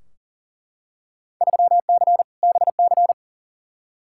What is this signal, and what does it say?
Morse code, heard as a keyed tone, about 40 wpm, 700 Hz. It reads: 3C BC
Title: Field Day Exchanges Encore - 40wpm